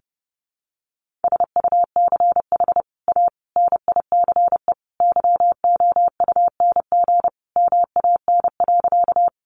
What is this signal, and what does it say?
Morse code, heard as a keyed tone, about 30 wpm, 700 Hz. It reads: SUCH A NICE YOUNG MAN.
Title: Sentences from Top 1100 Words 30wpm